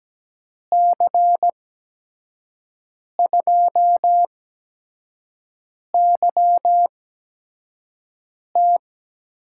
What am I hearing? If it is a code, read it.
Morse code, heard as a keyed tone, about 17 wpm, spaced as for 5 wpm, 700 Hz. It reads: C2YT
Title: USA Call Signs - 6 Characters - 8x Character Spacing - 17wpm